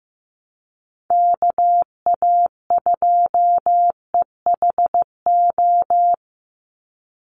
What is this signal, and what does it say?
Morse code, heard as a keyed tone, about 15 wpm, 700 Hz. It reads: KA2EHO
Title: USA Call Signs 15wpm